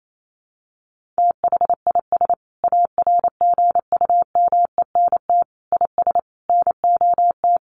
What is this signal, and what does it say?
Morse code, heard as a keyed tone, about 28 wpm, 700 Hz. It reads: THIS ARGUMENT IS NOT
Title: Sentences from Top 1400 Words 28wpm